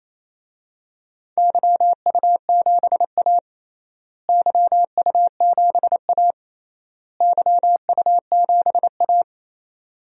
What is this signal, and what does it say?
Morse code, heard as a keyed tone, about 28 wpm, 700 Hz. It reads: YU7A YU7A YU7A
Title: DXCC-100 Call Signs - 4 Characters - Repeated 3x - 28wpm